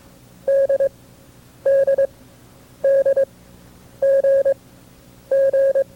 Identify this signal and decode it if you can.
Morse code, heard as a keyed tone, about 22 wpm, spaced as for 9 wpm, 540 Hz. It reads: DDDGG
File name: FreeFiles / CW-practice / KB6MT-tape2-side2